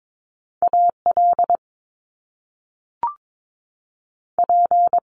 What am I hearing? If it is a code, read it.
Morse code, heard as a keyed tone, about 22 wpm, 700 Hz. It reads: AL P
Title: Top Two Letter Combinations in Words 22wpm